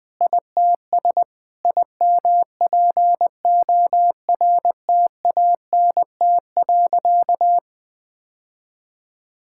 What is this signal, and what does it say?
Morse code, heard as a keyed tone, about 20 wpm, 700 Hz. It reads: ITS IMPORTANT.
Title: Sentences with 6-Word Phrases - 20wpm